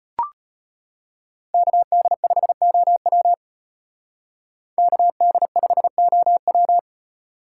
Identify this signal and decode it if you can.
Morse code, heard as a keyed tone, about 30 wpm, 700 Hz. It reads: KD5OW KD5OW
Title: Speed Racing - USA 2x2 Call Signs 25wpm